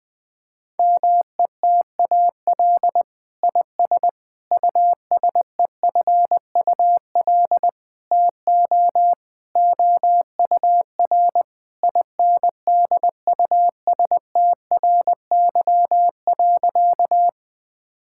Morse code, heard as a keyed tone, about 21 wpm, 700 Hz. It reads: METAL IS USEFUL TO OUR INDUSTRY.
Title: Sentences from Top 1000 Words - Review Part 6 - 20wpm